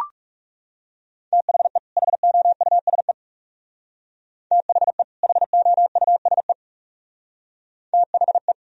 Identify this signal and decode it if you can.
Morse code, heard as a keyed tone, about 37 wpm, 700 Hz. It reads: THE HOUSE THE HOUSE THE
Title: Speed-Racing - Sets of 2 Words 30wpm